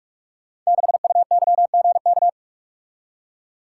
Morse code, heard as a keyed tone, about 44 wpm, 690 Hz. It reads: 6UYGK
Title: Code Groups - 5 Letter:Numbers - 45wpm